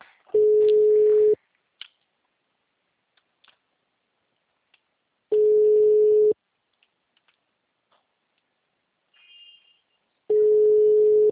Ringback tone (Europe, 425 Hz).
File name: out- -2035- - - .195